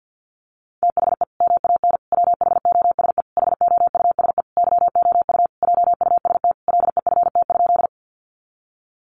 Morse code, heard as a keyed tone, about 50 wpm, 700 Hz. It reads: THE MAN WHOSE HOUSE YOU JUST LEFT?
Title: Sentences from Top 1200 Words 50wpm